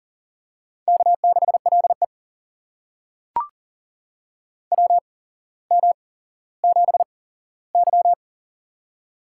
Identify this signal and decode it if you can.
Morse code, heard as a keyed tone, about 38 wpm, spaced as for 17 wpm, 700 Hz. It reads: K6LE WM7Y
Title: USA Call Signs - 4 Characters - 8x Character Spacing - 40wpm